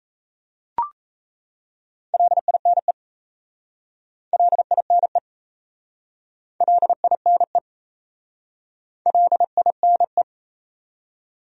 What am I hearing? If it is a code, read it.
Morse code, heard as a keyed tone, about 37 wpm, 700 Hz. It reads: LINE LINE LINE LINE